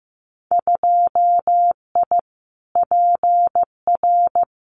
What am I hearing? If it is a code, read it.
Morse code, heard as a keyed tone, about 15 wpm, 700 Hz. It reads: 2I PR
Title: Winter Field Day Exchanges Encore 15wpm